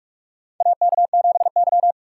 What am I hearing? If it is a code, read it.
Morse code, heard as a keyed tone, about 47 wpm, 700 Hz. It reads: AK7Y
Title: USA Call Signs - 4 Characters - 4x Character Spacing - 45wpm